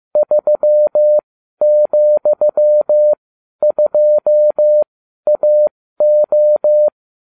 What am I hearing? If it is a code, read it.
Morse code, heard as a keyed tone, about 15 wpm, spaced as for 12 wpm, 600 Hz. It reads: 3,2AO